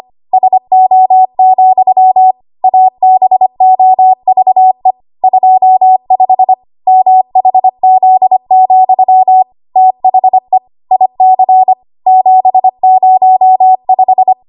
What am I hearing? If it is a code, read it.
Morse code, heard as a keyed tone, about 25 wpm, 750 Hz. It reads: SO, ABOVE 25 MHZ, THE IC 705
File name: Archive / 25 WPM / 25WPM